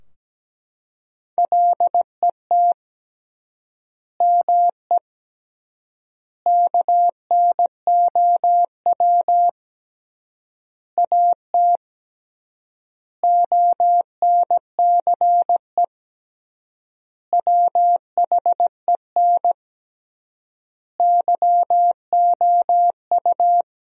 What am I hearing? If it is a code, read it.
Morse code, heard as a keyed tone, about 17 wpm, 700 Hz. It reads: LET ME KNOW AT ONCE WHEN YOU
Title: 3x Word-Spacing - Sentences from Top 300 Words - 17wpm